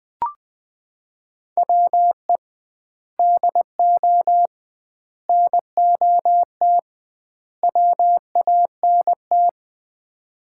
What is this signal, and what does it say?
Morse code, heard as a keyed tone, about 20 wpm, 700 Hz. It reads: WE DO NOT WANT
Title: 2x Word-Spacing - Sets of 4 Words - 20wpm